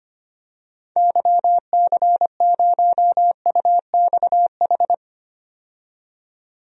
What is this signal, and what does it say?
Morse code, heard as a keyed tone, about 25 wpm, 700 Hz. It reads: YC0UXH
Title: DXCC-100 Call Signs - 6 Characters - 25wpm